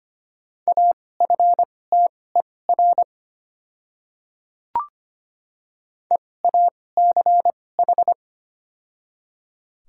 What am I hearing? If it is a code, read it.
Morse code, heard as a keyed tone, about 25 wpm, spaced as for 18 wpm, 690 Hz. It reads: AFTER EACH
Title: 2x Character Spacing - Top 200 Words - 25wpm